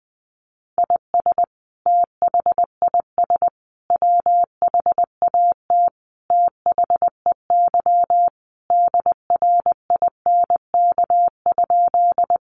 Morse code, heard as a keyed tone, about 20 wpm, 700 Hz. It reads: IS THIS WHAT THEY DRINK?
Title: Sentences from Top 1500 Words 20wpm